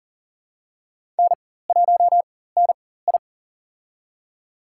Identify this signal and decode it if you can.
Morse code, heard as a keyed tone, about 40 wpm, spaced as for 19 wpm, 700 Hz. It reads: N1NI
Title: USA Call Signs - 4x Character Spacing - 40wpm